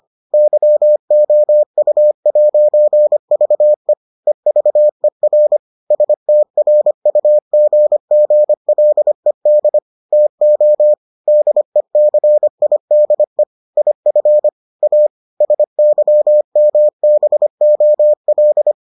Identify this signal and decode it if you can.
Morse code, heard as a keyed tone, about 25 wpm, 600 Hz. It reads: YOU'VE EVER STRUGGLED TO DECIDE IF A SYMBOL